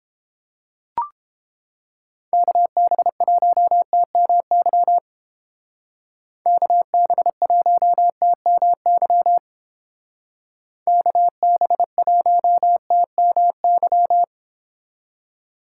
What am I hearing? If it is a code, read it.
Morse code, heard as a keyed tone, about 30 wpm, 700 Hz. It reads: KB1TMY KB1TMY KB1TMY